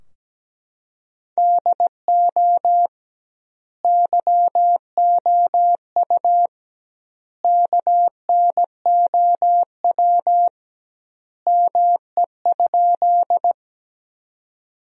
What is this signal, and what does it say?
Morse code, heard as a keyed tone, about 17 wpm, 700 Hz. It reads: DO YOU KNOW ME?
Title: 2x Word-Spacing - Sentences from Top 200 Words - 17wpm